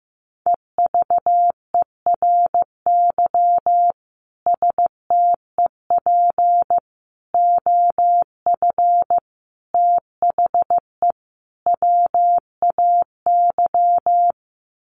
Morse code, heard as a keyed tone, about 15 wpm, 700 Hz. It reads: EVERY STEP OF THE WAY
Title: Sets of 5 Words Encore2 15wpm